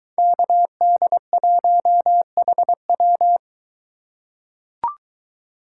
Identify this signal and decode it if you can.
Morse code, heard as a keyed tone, about 23 wpm, 700 Hz. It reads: KD1HW